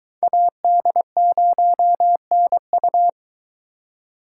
Morse code, heard as a keyed tone, about 22 wpm, 700 Hz. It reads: AD0NU